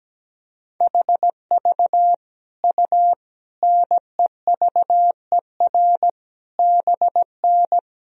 Morse code, heard as a keyed tone, about 17 wpm, 700 Hz. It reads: HV U NEVER BN
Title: Sentences from Top 300 Words and Abbreviations 17wpm